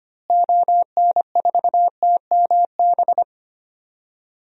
Morse code, heard as a keyed tone, about 25 wpm, 700 Hz. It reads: ON4TMB
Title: DXCC-100 Call Signs - 6 Characters - Repeated 2x - 25wpm